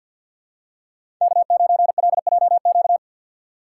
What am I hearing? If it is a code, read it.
Morse code, heard as a keyed tone, about 45 wpm, 700 Hz. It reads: K9LJQ
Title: USA Call Signs - 5 Characters - 2x Character Spacing - 50wpm